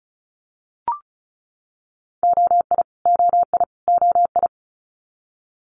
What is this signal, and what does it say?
Morse code, heard as a keyed tone, about 36 wpm, 700 Hz. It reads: OI OI OI
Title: Multi-letter Phonemes Repeated 3x 35wpm